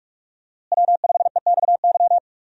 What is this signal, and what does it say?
Morse code, heard as a keyed tone, about 47 wpm, 700 Hz. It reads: W5EXY
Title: USA Call Signs - 5 Characters - Repeated 2x - 45wpm